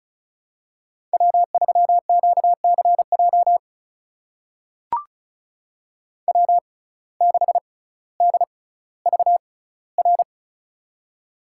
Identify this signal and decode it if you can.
Morse code, heard as a keyed tone, about 34 wpm, spaced as for 18 wpm, 700 Hz. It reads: W3QCJ W6DVR
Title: USA Call Signs - 5 Characters - 6x Character Spacing - 35wpm